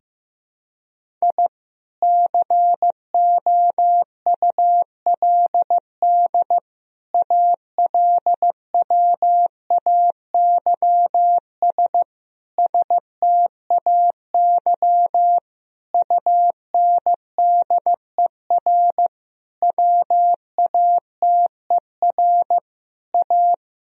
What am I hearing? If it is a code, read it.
Morse code, heard as a keyed tone, about 15 wpm, 700 Hz. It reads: I COULD ALWAYS STAY UNDER WATER A
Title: Sentences from Top 1000 Words - Review Part 1 - 15wpm